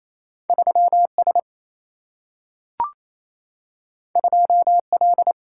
Morse code, heard as a keyed tone, about 28 wpm, 700 Hz. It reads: 3 S 2 L
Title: Single-Letter Single-Number 28wpm